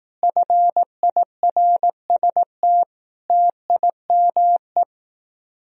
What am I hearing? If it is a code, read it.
Morse code, heard as a keyed tone, about 18 wpm, 700 Hz. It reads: FIRST TIME
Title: Speed-Racing - Sets of 5 Words 15wpm